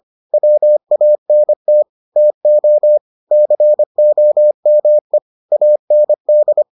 Morse code, heard as a keyed tone, about 25 wpm, 600 Hz. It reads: WANT TO COME AND